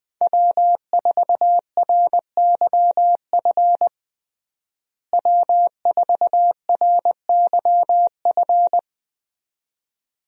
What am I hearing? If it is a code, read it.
Morse code, heard as a keyed tone, about 20 wpm, 700 Hz. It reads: W4RYF W4RYF
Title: USA Call Signs - 5 Characters - Repeated 2x - 20wpm